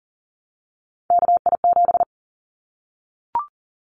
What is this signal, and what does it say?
Morse code, heard as a keyed tone, about 40 wpm, 700 Hz. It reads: KI7
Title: Callsign Prefixes for Top 25 DXCC - Repeated 3x - 40wpm